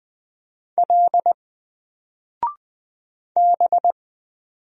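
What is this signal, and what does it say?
Morse code, heard as a keyed tone, about 20 wpm, 700 Hz. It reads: L B
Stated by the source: Single letter and number Morse drill at 20 wpm, from N4LCD.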